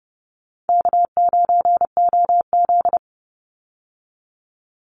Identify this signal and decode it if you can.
Morse code, heard as a keyed tone, about 30 wpm, 700 Hz. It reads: K9OZ
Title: Easy Call Signs Repeated 2x - 30wpm